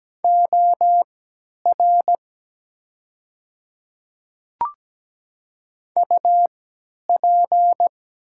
Morse code, heard as a keyed tone, about 17 wpm, spaced as for 10 wpm, 700 Hz. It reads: OR UP
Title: 3x Character Spacing - Top 100 Words - 17wpm